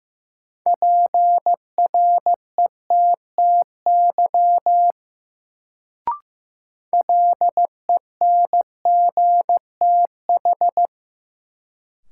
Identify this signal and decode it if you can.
Morse code, heard as a keyed tone, about 15 wpm, 700 Hz. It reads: PRETTY LENGTH
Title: Sentences from Top 1000 Words - New Word Drill - 15wpm